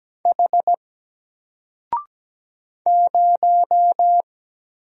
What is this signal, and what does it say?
Morse code, heard as a keyed tone, about 17 wpm, 700 Hz. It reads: H 0
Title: Single Letter-Number Alt 17wpm